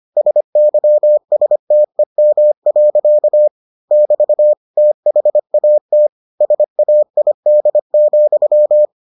Morse code, heard as a keyed tone, about 25 wpm, 600 Hz. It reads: SYSTEM. <BT> THAT SAID,